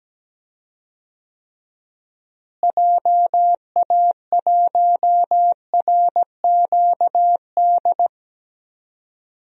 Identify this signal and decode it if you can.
Morse code, heard as a keyed tone, about 17 wpm, 700 Hz. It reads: JA1RQD
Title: DXCC-100 Call Signs - 6 Characters - Repeated 3x - 17wpm